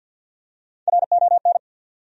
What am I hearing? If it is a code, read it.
Morse code, heard as a keyed tone, about 52 wpm, 700 Hz. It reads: RON
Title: USA Common Names - Max 4 Letters - 50wpm